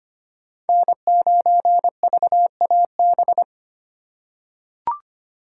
Morse code, heard as a keyed tone, about 25 wpm, 700 Hz. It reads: N9VAB
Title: USA Call Signs - 5 Characters - 2x Character Spacing - 25wpm